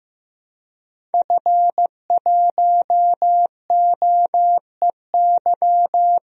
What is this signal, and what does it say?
Morse code, heard as a keyed tone, about 15 wpm, 700 Hz. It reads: F1OEY
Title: DXCC-100 Call Signs - 5 Characters - Repeated 3x - 15wpm